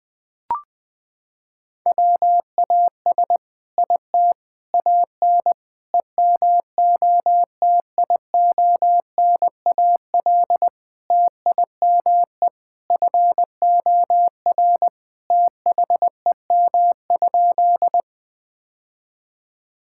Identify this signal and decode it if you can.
Morse code, heard as a keyed tone, about 20 wpm, 700 Hz. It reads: WAS IT AN EMOTIONAL TIME FOR THEM?
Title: Sentences from Top 1000 Words - Review Part 5 - 20wpm